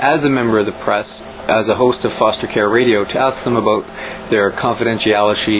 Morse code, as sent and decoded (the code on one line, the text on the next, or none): none